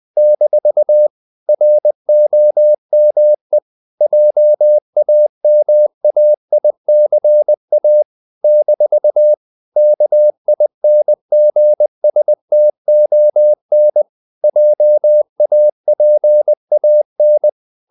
Morse code, heard as a keyed tone, about 20 wpm, 600 Hz. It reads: - ROME JAMAICA - KINGSTON JAPAN
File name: CW / Practice / NationalCapitals20